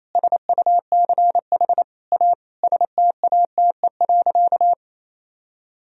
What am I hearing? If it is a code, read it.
Morse code, heard as a keyed tone, about 28 wpm, 700 Hz. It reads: SUCH A STATE.
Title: Sentences from Top 1000 Words - Review Part 2 - 28wpm